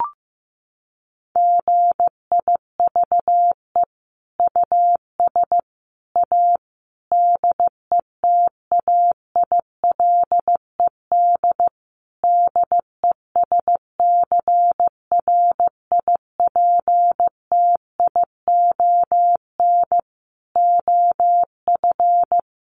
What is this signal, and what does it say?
Morse code, heard as a keyed tone, about 14 wpm, 700 Hz. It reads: GIVE US A DETAILED DESCRIPTION OF
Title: Sentences from Top 1500 Words 15wpm